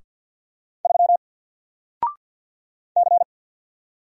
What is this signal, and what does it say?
Morse code, heard as a keyed tone, about 48 wpm, 700 Hz. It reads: 3 C
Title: Single Character (Letter:Number:Punctuation:Prosign) Alt 50wpm